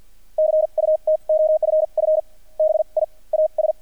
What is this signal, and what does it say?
Morse code, heard as a keyed tone, about 48 wpm, spaced as for 36 wpm, 620 Hz. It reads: CUT OFF BIAS
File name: archived / CRSnet